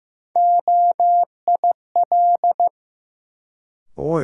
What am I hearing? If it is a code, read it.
Morse code, heard as a keyed tone, about 15 wpm, 700 Hz. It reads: OIL
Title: Sentences from Top 1000 Words - Review Part 4 - New Word Drill - 15wpm